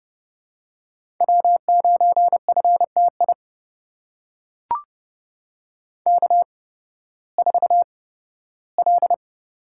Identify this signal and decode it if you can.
Morse code, heard as a keyed tone, about 30 wpm, spaced as for 16 wpm, 700 Hz. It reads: W9FTI K4L
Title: USA Call Signs - 5 Characters - 8x Character Spacing - 30wpm